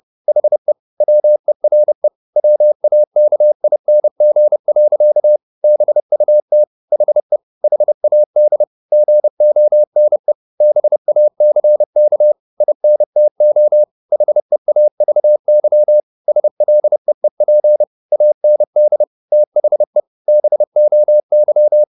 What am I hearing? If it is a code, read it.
Morse code, heard as a keyed tone, about 30 wpm, 600 Hz. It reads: HE WERE WAKING. BUT HE HAD GONE BACK INTO HEAVY SLEEP AND THE BOY